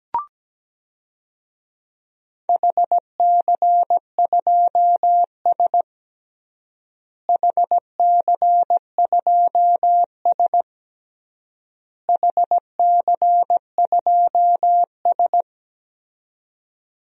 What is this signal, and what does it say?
Morse code, heard as a keyed tone, about 17 wpm, 700 Hz. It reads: HC2S HC2S HC2S
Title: DXCC-100 Call Signs - 4 Characters - Repeated 3x - 17wpm